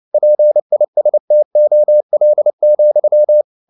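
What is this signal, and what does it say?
Morse code, heard as a keyed tone, about 29 wpm, 600 Hz. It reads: PISTOL,
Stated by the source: Morse code at 29 wpm, sending Ken Follett's novel Never in English, from CW med Gnister.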